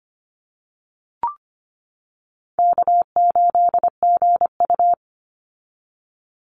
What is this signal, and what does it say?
Morse code, heard as a keyed tone, about 25 wpm, 700 Hz. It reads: K8GU
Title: Easy Call Signs - 25wpm